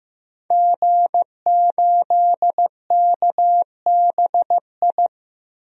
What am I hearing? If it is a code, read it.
Morse code, heard as a keyed tone, about 15 wpm, 700 Hz. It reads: G8KBI
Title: DXCC-100 Call Signs - 5 Characters - 15wpm